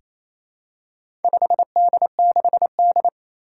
Morse code, heard as a keyed tone, about 27 wpm, 700 Hz. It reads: 5D6D
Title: Four Challenging Letter-Number Combinations 28wpm